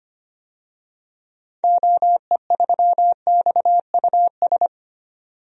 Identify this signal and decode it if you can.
Morse code, heard as a keyed tone, about 25 wpm, 690 Hz. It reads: OE3XUS
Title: DXCC-100 Call Signs - 6 Characters - 25wpm